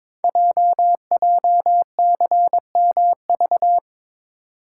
Morse code, heard as a keyed tone, about 22 wpm, 700 Hz. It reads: JJCMV
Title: Code Groups - 5 Letters - 22wpm